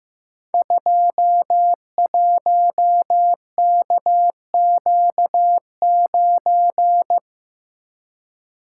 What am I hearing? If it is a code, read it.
Morse code, heard as a keyed tone, about 15 wpm, 700 Hz. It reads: 21KQ9
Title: Code Groups - 5 Letter:Numbers - 15wpm